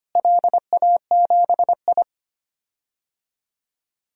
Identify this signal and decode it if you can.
Morse code, heard as a keyed tone, about 25 wpm, 700 Hz. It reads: LA7I
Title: DXCC-100 Call Signs - 4 Characters - Repeated 3x - 25wpm